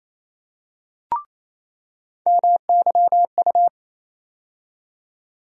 Morse code, heard as a keyed tone, about 27 wpm, 700 Hz. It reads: MYU
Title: USA Call Sign Postfixes - 3 Letters 28wpm